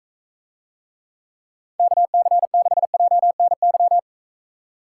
Morse code, heard as a keyed tone, about 45 wpm, 700 Hz. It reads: KC6JNY